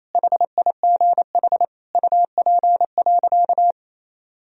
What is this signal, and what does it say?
Morse code, heard as a keyed tone, about 28 wpm, 700 Hz. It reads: HIGH UP.
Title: Sentences from Top 1000 Words - Review Part 1 - 28wpm